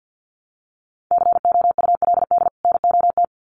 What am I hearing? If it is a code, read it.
Morse code, heard as a keyed tone, about 50 wpm, 700 Hz. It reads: COULD NOT
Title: Sets of 2 Words 50wpm